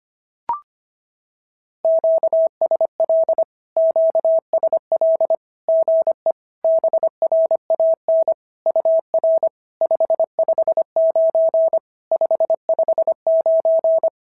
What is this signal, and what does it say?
Morse code, heard as a keyed tone, about 25 wpm, 650 Hz. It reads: QSL QSL GE BRAN UR 559 559
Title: SOTA QSO Practice 25wpm